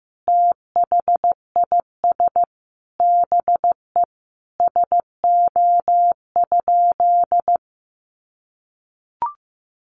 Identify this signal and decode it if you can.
Morse code, heard as a keyed tone, about 15 wpm, 700 Hz. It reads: THIS BE SO?
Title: Sentences from Top 300 Words 15wpm